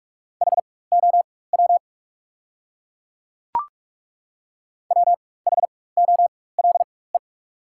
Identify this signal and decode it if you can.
Morse code, heard as a keyed tone, about 42 wpm, spaced as for 21 wpm, 690 Hz. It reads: HOW WHOLE
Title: 4x Character Spacing - Top 500 Words - 45wpm